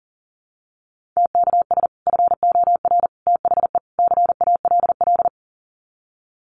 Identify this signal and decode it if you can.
Morse code, heard as a keyed tone, about 40 wpm, 700 Hz. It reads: TKS FOR THE CALL
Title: Common QSO Phrases 40wpm